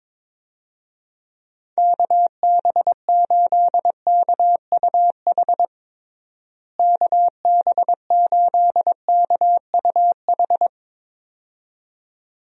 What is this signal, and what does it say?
Morse code, heard as a keyed tone, about 21 wpm, 700 Hz. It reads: KB8KUH KB8KUH
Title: USA Call Signs - 6 Characters - Repeated 2x - 22wpm